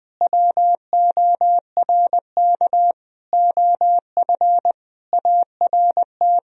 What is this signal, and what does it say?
Morse code, heard as a keyed tone, about 20 wpm, 700 Hz. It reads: WORK OF ART
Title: Sentences from Top 1000 Words - Review Part 3 - 20wpm